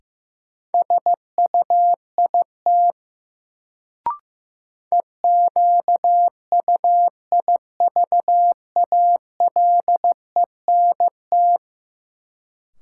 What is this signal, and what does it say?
Morse code, heard as a keyed tone, about 15 wpm, 700 Hz. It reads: SUIT EQUIVALENT
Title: Sentences from Top 1600 Words - New Word Drill - 15wpm